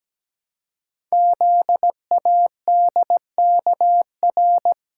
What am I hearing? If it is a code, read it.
Morse code, heard as a keyed tone, about 17 wpm, 700 Hz. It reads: ZADKR
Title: Code Groups - 5 Letters - 17wpm